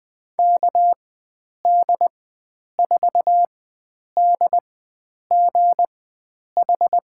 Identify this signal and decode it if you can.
Morse code, heard as a keyed tone, about 20 wpm, spaced as for 9 wpm, 700 Hz. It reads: KD4DGH
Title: USA Call Signs - 6 Characters - 4x Character Spacing - 20wpm